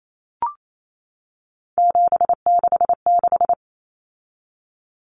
Morse code, heard as a keyed tone, about 28 wpm, 700 Hz. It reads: 766
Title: Common Cut Numbers - 3-Digit 28wpm